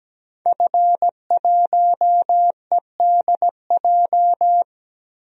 Morse code, heard as a keyed tone, about 17 wpm, 700 Hz. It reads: F1EDJ
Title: DXCC-100 Call Signs - 5 Characters - 17wpm